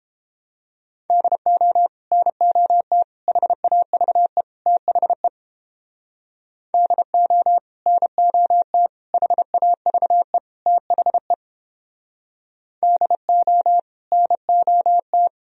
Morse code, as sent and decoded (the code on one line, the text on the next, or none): -.. --- / -. --- - / .... .- ...- . / - .... / -.. --- / -. --- - / .... .- ...- . / - .... / -.. --- / -. --- -
DO NOT HAVE THE DO NOT HAVE THE DO NOT